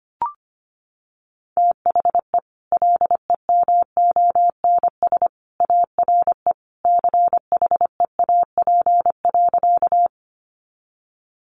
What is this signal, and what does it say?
Morse code, heard as a keyed tone, about 25 wpm, 700 Hz. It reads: THE LEMONS ARE CHEAP.
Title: Additional Easy Phrases 25wpm